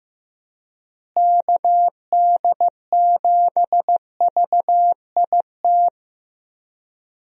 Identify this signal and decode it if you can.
Morse code, heard as a keyed tone, about 15 wpm, 700 Hz. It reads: KD7VIT
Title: USA Call Signs - 6 Characters - Repeated 2x - 15wpm